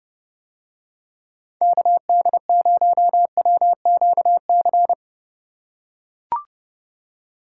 Morse code, heard as a keyed tone, about 28 wpm, 700 Hz. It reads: KD0WQC